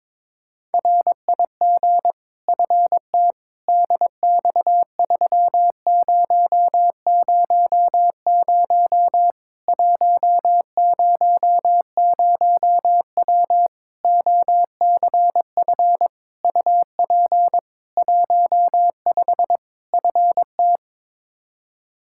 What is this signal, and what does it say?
Morse code, heard as a keyed tone, about 22 wpm, 700 Hz. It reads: RIG FT DX3000 100W OCF UP 15 FT
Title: K1USN SST QSO Practice 22wpm